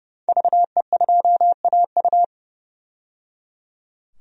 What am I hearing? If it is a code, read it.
Morse code, heard as a keyed tone, about 30 wpm, 700 Hz. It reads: VE2AU
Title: DXCC-100 Call Signs - 5 Characters - 30wpm